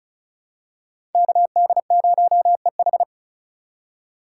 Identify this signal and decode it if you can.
Morse code, heard as a keyed tone, about 36 wpm, 700 Hz. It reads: KD0EH
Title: USA Call Signs - 4x Character Spacing - 35wpm